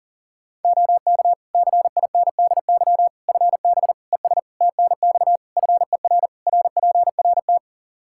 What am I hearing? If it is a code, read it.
Morse code, heard as a keyed tone, about 40 wpm, 700 Hz. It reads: OK CINDY FB ES TNX FER RPRT
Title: Practice QSO - Exchange 2 - Info - 40wpm